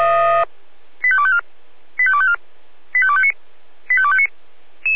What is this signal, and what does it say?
Five-tone selective call (ZVEI), heard as quick runs of five tones, tones of about 70 ms. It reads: 85254 85254 85259 85259